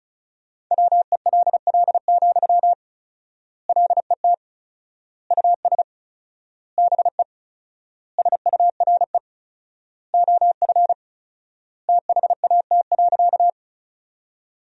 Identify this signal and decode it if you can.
Morse code, heard as a keyed tone, about 34 wpm, 700 Hz. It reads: WELL, LET US BE SURE OF THAT.
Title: 4x Word-Spacing - Sentences from Top 300 Words - 35wpm